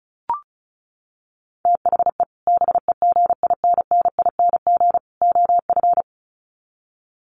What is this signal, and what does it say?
Morse code, heard as a keyed tone, about 34 wpm, 700 Hz. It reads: THE BEGINNING OF
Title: Sets of 3 Words 35wpm